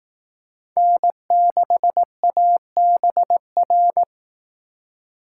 Morse code, heard as a keyed tone, about 19 wpm, 700 Hz. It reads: N6ABR
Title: Speed Racing - USA 1x3 Call Signs 15wpm